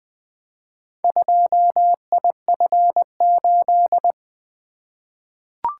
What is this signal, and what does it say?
Morse code, heard as a keyed tone, about 20 wpm, 700 Hz. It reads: 2IF8
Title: Four Letter-Number Combinations - 20wpm